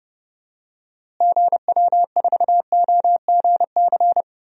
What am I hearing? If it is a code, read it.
Morse code, heard as a keyed tone, about 32 wpm, 700 Hz. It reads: GW4OGC
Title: DXCC-100 Call Signs - 6 Characters - Repeated 3x - 30wpm